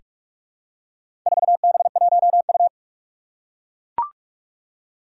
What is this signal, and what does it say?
Morse code, heard as a keyed tone, about 44 wpm, 700 Hz. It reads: 4B1U
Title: Four Challenging Letter-Number Combinations 45wpm